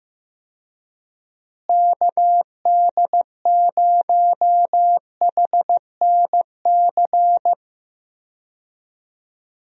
Morse code, heard as a keyed tone, about 15 wpm, 700 Hz. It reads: KD0HNC
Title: USA Call Signs - 6 Characters - Repeated 2x - 15wpm